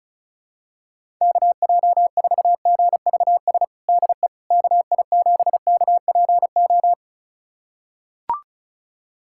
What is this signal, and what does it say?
Morse code, heard as a keyed tone, about 36 wpm, 700 Hz. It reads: KJ4GVS DE KI7KPO
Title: USA Callsign DE Callsign - 6 Characters - 35wpm